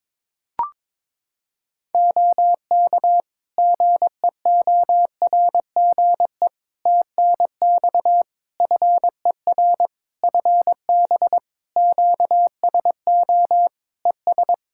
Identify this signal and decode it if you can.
Morse code, heard as a keyed tone, about 22 wpm, 700 Hz. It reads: OK GEORGE TNX FER FB QSO ES
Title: K1USN SST QSO Practice 22wpm